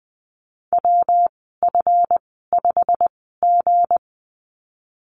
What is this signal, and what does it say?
Morse code, heard as a keyed tone, about 20 wpm, spaced as for 15 wpm, 700 Hz. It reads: WF5G
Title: 179 Lesson 29g - Slash - Call Signs 20wpm